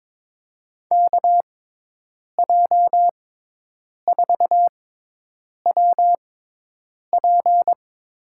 Morse code, heard as a keyed tone, about 22 wpm, spaced as for 8 wpm, 700 Hz. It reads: KJ4WP
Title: USA Call Signs - 6 Characters - 6x Character Spacing - 22wpm